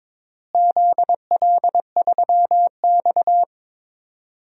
Morse code, heard as a keyed tone, about 21 wpm, 700 Hz. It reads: ZL3X
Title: DXCC-100 Call Signs - 4 Characters - Repeated 2x - 22wpm